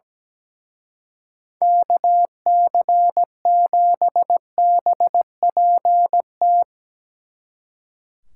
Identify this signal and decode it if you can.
Morse code, heard as a keyed tone, about 19 wpm, 700 Hz. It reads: KC7BPT
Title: Speed Racing - USA Call Signs 17wpm